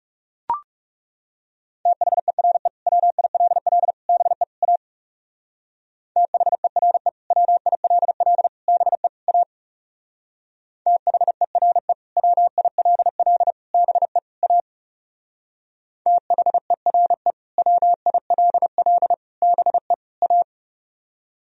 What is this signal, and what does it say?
Morse code, heard as a keyed tone, about 36 wpm, 700 Hz. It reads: THERE WILL BE A THERE WILL BE A THERE WILL BE A THERE WILL BE A
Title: Speed-Racing - Sets of 4 Words 30wpm